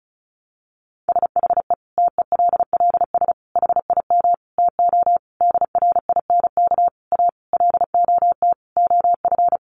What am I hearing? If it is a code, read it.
Morse code, heard as a keyed tone, about 34 wpm, 700 Hz. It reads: SHE TELLS HIM TO DRINK A LOT OF
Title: Additional Easy Phrases 35wpm